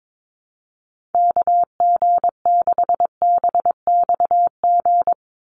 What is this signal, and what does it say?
Morse code, heard as a keyed tone, about 22 wpm, 700 Hz. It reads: KG6BXG
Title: USA Call Signs 22wpm